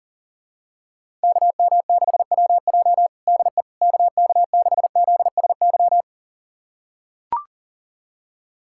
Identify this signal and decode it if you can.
Morse code, heard as a keyed tone, about 40 wpm, 700 Hz. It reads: KM6WJ DE KK6ZSY
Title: USA Callsign DE Callsign 40wpm